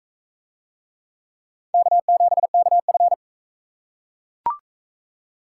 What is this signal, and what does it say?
Morse code, heard as a keyed tone, about 38 wpm, 700 Hz. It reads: K7KF